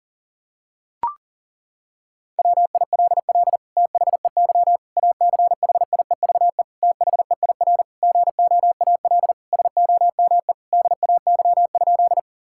Morse code, heard as a keyed tone, about 40 wpm, 700 Hz. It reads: WILL THEY ACHIEVE THEIR GOAL SOME DAY?
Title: Sentences from Top 1000 Words - Review Part 6 - 40wpm